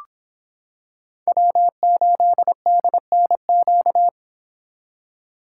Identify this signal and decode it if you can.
Morse code, heard as a keyed tone, about 26 wpm, 700 Hz. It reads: W8DNQ